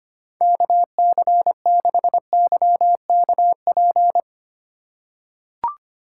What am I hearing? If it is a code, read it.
Morse code, heard as a keyed tone, about 25 wpm, 700 Hz. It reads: KC6YKP
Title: USA Call Signs - 2x Character Spacing - 25wpm